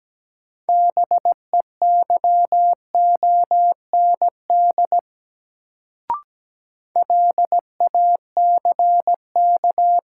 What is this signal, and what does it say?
Morse code, heard as a keyed tone, about 17 wpm, 700 Hz. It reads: BEYOND LACK
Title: Sentences from Top 1100 Words - New Word Drill Alt - 17wpm